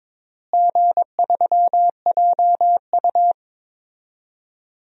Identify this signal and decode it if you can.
Morse code, heard as a keyed tone, about 22 wpm, 700 Hz. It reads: G3JU
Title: DXCC-100 Call Signs - 4 Characters - Repeated 2x - 22wpm